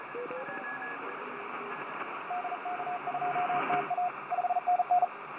Morse code, heard as a keyed tone, about 36 wpm, 720 Hz, with radio noise. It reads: DK2TA 5NN
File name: 6o0cw-on-10m-cw